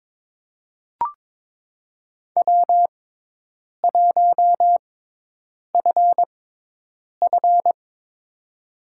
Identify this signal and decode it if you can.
Morse code, heard as a keyed tone, about 22 wpm, spaced as for 8 wpm, 700 Hz. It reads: W1FF
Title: USA Call Signs - 6x Character Spacing - 22wpm